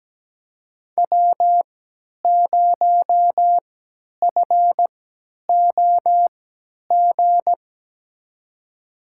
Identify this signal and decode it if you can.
Morse code, heard as a keyed tone, about 17 wpm, spaced as for 10 wpm, 700 Hz. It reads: W0FOG
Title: USA Call Signs - 5 Characters - 3x Character Spacing - 17wpm